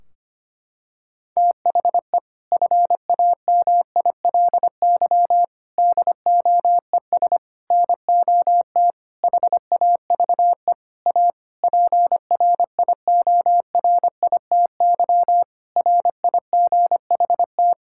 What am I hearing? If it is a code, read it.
Morse code, heard as a keyed tone, about 25 wpm, 700 Hz. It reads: THE FAMILY DOES NOT HAVE A PRIORITY RIGHT
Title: Sentences from Top 1600 Words 25wpm